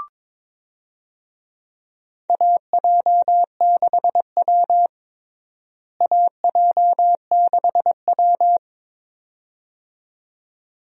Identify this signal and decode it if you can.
Morse code, heard as a keyed tone, about 21 wpm, 700 Hz. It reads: AJ6W AJ6W
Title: USA Call Signs - 4 Characters - Repeated 2x - 22wpm